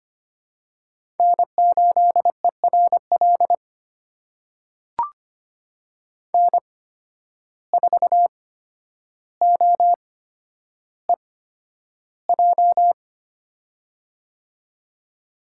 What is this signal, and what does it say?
Morse code, heard as a keyed tone, about 25 wpm, spaced as for 11 wpm, 700 Hz. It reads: N8ERL N4OEJ
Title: USA Call Signs - 5 Characters - 8x Character Spacing - 25wpm